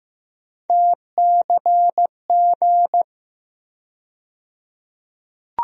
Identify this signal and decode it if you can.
Morse code, heard as a keyed tone, about 15 wpm, 700 Hz. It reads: TCG
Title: USA Call Sign Postfixes - 3 Letters 15wpm